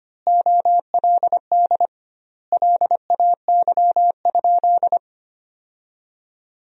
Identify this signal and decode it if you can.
Morse code, heard as a keyed tone, about 25 wpm, 700 Hz. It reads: OLD LAY?
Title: 2x Word-Spacing - Sentences from Top 400 Words - 25wpm